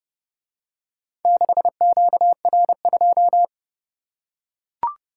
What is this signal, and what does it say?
Morse code, heard as a keyed tone, about 28 wpm, 700 Hz. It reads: 6QR2